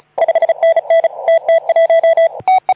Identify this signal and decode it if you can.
Morse code, heard as a keyed tone, about 34 wpm, 660 Hz. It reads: 5NN TT1
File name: Archiv / CUC181 / OK2ZC CUC 181